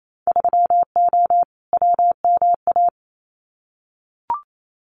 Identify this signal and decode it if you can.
Morse code, heard as a keyed tone, about 28 wpm, 700 Hz. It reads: 3O WMA
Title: Winter Field Day Exchanges Encore 28wpm